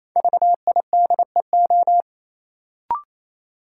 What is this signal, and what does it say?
Morse code, heard as a keyed tone, about 28 wpm, 700 Hz. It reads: VIDEO
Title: Sentences from Top 1000 Words - Review Part 4 - New Word Drill - 28wpm